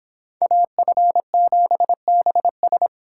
Morse code, heard as a keyed tone, about 25 wpm, 700 Hz. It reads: AF7BS